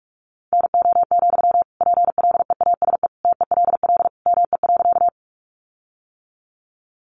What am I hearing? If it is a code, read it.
Morse code, heard as a keyed tone, about 44 wpm, 700 Hz. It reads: NO, PLEASE TELL ME.
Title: Sentences from Top 1500 Words 45wpm